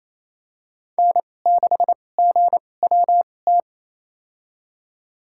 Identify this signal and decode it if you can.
Morse code, heard as a keyed tone, about 28 wpm, spaced as for 20 wpm, 700 Hz. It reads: N6GWT
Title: USA Call Signs - 5 Characters - 2x Character Spacing - 28wpm